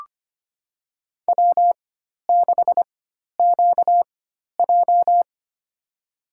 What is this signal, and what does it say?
Morse code, heard as a keyed tone, about 25 wpm, spaced as for 12 wpm, 700 Hz. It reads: W6QJ
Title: USA Call Signs - 4x Character Spacing - 25wpm